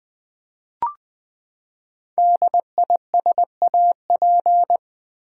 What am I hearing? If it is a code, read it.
Morse code, heard as a keyed tone, about 20 wpm, 700 Hz. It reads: DISAP
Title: Build A Word - 5 Letters At A Time - 10-Letter Words - 20wpm